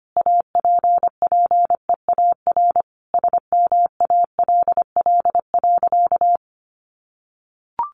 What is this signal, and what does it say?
Morse code, heard as a keyed tone, about 25 wpm, 700 Hz. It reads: APPEAR SMALL.
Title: Sentences from Top 1500 Words 25wpm